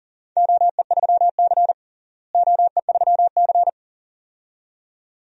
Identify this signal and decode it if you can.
Morse code, heard as a keyed tone, about 40 wpm, 700 Hz. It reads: OE3C OE3C
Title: DXCC-100 Call Signs - 4 Characters - Repeated 2x - 40wpm